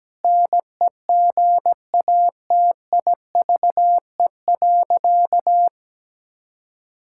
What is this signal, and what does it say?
Morse code, heard as a keyed tone, about 17 wpm, 700 Hz. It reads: NEGATIVE.
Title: Sentences from Top 1000 Words - Review Part 6 - 17wpm